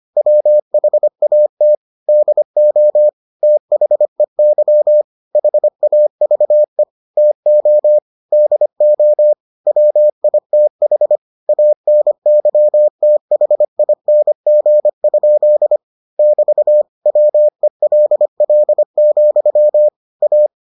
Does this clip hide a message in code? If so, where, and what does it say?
Yes: Morse code from 0.2 to 20.5 s, heard as a keyed tone, about 25 wpm, 600 Hz. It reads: WHAT DO THEY HAVE TO DO WITH ANYTHING? <BT> WELL, A